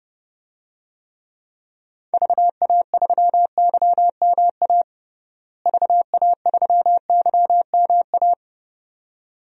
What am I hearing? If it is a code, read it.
Morse code, heard as a keyed tone, about 30 wpm, 700 Hz. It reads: VA3YMA VA3YMA
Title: DXCC-100 Call Signs - 6 Characters - Repeated 2x - 30wpm